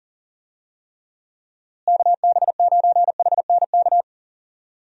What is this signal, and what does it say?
Morse code, heard as a keyed tone, about 40 wpm, 700 Hz. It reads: KB9HNK
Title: USA Call Signs - 6 Characters - Repeated 3x - 40wpm